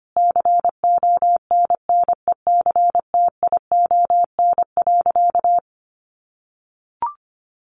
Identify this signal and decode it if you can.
Morse code, heard as a keyed tone, about 25 wpm, 700 Hz. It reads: CONNECTION.
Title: Sentences from Top 1300 Words 25wpm